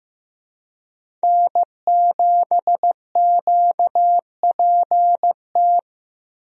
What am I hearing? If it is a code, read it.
Morse code, heard as a keyed tone, about 15 wpm, 700 Hz. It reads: N7QPT
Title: USA Call Signs - 5 Characters - Repeated 3x - 15wpm